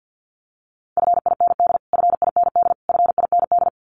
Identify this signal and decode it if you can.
Morse code, heard as a keyed tone, about 50 wpm, 700 Hz. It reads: FIND FIND FIND
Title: Top 500 Words Repeated 50wpm